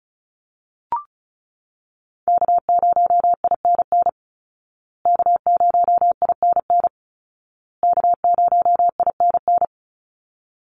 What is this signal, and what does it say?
Morse code, heard as a keyed tone, about 34 wpm, 700 Hz. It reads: K0INN K0INN K0INN
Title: Call Signs Repeated 2x - 35wpm